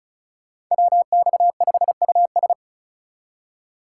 Morse code, heard as a keyed tone, about 44 wpm, 700 Hz. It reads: WX5US
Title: Speed Racing - USA 2x2 Call Signs 35wpm